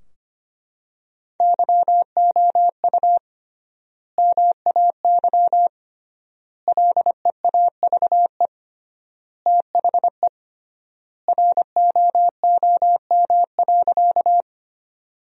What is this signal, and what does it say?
Morse code, heard as a keyed tone, about 25 wpm, 700 Hz. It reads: YOU MAY LEAVE THE ROOM.
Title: 3x Word-Spacing - Sentences from Top 400 Words - 25wpm